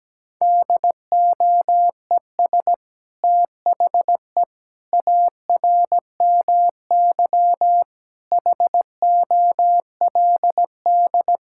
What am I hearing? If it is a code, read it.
Morse code, heard as a keyed tone, about 17 wpm, 700 Hz. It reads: DOES THE ARMY HOLD
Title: Sentences from Top 1000 Words - Review Part 6 - 17wpm